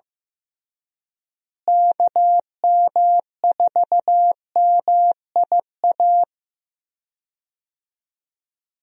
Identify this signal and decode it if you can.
Morse code, heard as a keyed tone, about 15 wpm, 700 Hz. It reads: KM4MIA
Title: USA Call Signs - 6 Characters - Repeated 3x - 15wpm